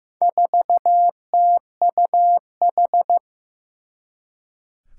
Morse code, heard as a keyed tone, about 15 wpm, 700 Hz. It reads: 4TUH
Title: Four Challenging Letter-Number Combinations 15wpm